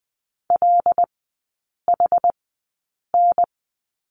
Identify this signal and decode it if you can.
Morse code, heard as a keyed tone, about 20 wpm, spaced as for 8 wpm, 700 Hz. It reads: LHN
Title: 074 Lesson 13e - Letter L - Three Characters 20wpm